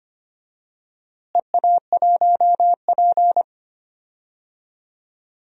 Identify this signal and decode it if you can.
Morse code, heard as a keyed tone, about 25 wpm, 700 Hz. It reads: EA1P
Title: DXCC-100 Call Signs - 4 Characters - Repeated 3x - 25wpm